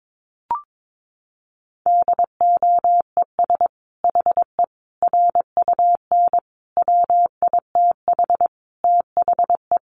Morse code, heard as a keyed tone, about 22 wpm, 700 Hz. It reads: DOES HE RUN WITH THE
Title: Sentences from Top 1300 Words 22wpm